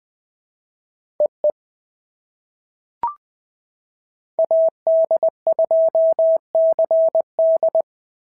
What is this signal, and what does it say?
Morse code, heard as a keyed tone, about 19 wpm, 650 Hz. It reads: EE AD2CD
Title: POTA QSO Practice - 20wpm